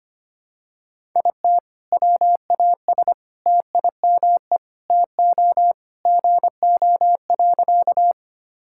Morse code, heard as a keyed tone, about 25 wpm, 700 Hz. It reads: IT WAS TIME TO GO.
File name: practice / 25wpm / Sentences with 4-Word Phrases - 25wpm